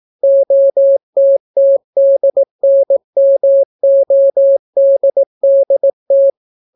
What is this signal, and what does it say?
Morse code, heard as a keyed tone, about 18 wpm, 550 Hz. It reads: OTTDNMODDT